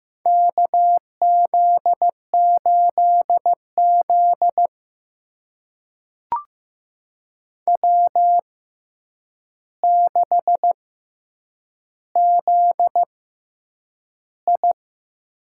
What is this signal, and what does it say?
Morse code, heard as a keyed tone, about 15 wpm, spaced as for 8 wpm, 700 Hz. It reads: KZ8Z W6ZI
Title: USA Call Signs - 4 Characters - 6x Character Spacing - 15wpm